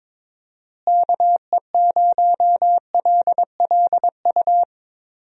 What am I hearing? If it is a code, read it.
Morse code, heard as a keyed tone, about 22 wpm, 700 Hz. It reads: KE0LLU
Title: USA Call Signs - 6 Characters - Repeated 3x - 22wpm